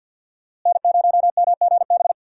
Morse code, heard as a keyed tone, about 47 wpm, 700 Hz. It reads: N0MGB
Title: USA Call Signs - 5 Characters - Repeated 3x - 50wpm